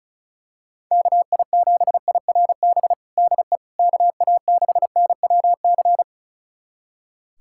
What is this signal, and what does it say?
Morse code, heard as a keyed tone, about 36 wpm, 700 Hz. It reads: KI7IRB DE KA6NWC
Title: USA Callsign DE Callsign - 6 Characters - 35wpm